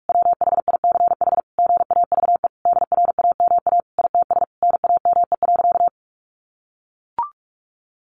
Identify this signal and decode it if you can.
Morse code, heard as a keyed tone, about 44 wpm, 700 Hz. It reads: WHICH GAVE DRAMA ITS NAME.
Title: Sentences from Top 1500 Words 45wpm